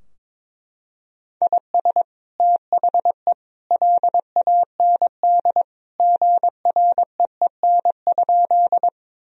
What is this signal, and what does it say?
Morse code, heard as a keyed tone, about 21 wpm, 700 Hz. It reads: IS THE LAND GREEN?
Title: Sentences from Top 1000 Words - Review Part 2 - 22wpm